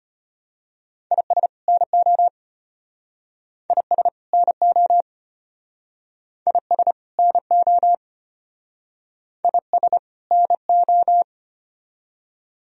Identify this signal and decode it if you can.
Morse code, heard as a keyed tone, about 34 wpm, 700 Hz. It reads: IS NO IS NO IS NO IS NO